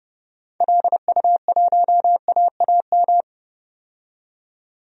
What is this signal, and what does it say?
Morse code, heard as a keyed tone, about 30 wpm, 700 Hz. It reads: LU1AAM
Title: DXCC-100 Call Signs - 6 Characters - 30wpm